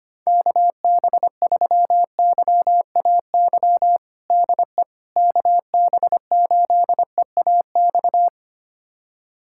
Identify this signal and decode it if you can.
Morse code, heard as a keyed tone, about 25 wpm, 700 Hz. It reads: KB3YAY DE KB8EAX
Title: USA Callsign DE Callsign - 6 Characters - 25wpm